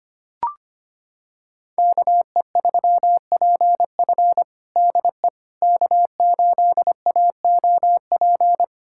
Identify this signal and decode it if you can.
Morse code, heard as a keyed tone, about 25 wpm, 700 Hz. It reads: KE3PF DE K8AOP
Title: USA Callsign DE Callsign - 5 Characters - 25wpm